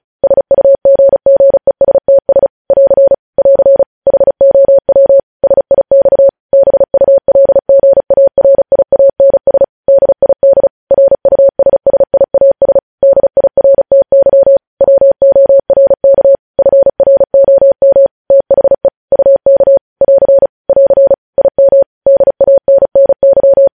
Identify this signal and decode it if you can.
Morse code, heard as a keyed tone, about 36 wpm, 550 Hz. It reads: SUGGESTS <AR> <AR> HOW SIX BULGARIANS DID RUSSIAS DIRTY WORK FROM THE UK <AR> <AR> IM DANNY
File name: podcast / News headlines-1x-35x35